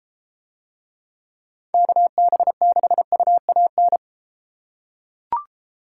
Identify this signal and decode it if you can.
Morse code, heard as a keyed tone, about 32 wpm, 700 Hz. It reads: KB6UAN